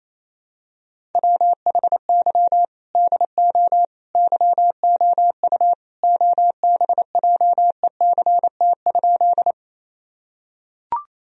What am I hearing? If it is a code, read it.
Morse code, heard as a keyed tone, about 28 wpm, 700 Hz. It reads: WHY DO YOU OBJECT?
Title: Sentences from Top 1000 Words - Review Part 3 - 28wpm